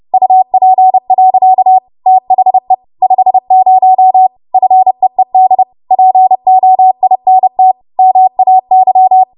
Morse code, heard as a keyed tone, about 30 wpm, 750 Hz. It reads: UP. THE 50 FEED POINT MAY
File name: Morse / 30WPM